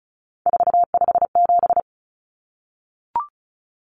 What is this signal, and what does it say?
Morse code, heard as a keyed tone, about 34 wpm, 700 Hz. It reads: 457
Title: Common Cut Numbers - 3-Digit 35wpm